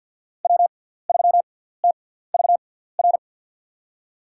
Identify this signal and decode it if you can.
Morse code, heard as a keyed tone, about 49 wpm, spaced as for 17 wpm, 700 Hz. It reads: W3TVR
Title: USA Call Signs - 5 Characters - 6x Character Spacing - 50wpm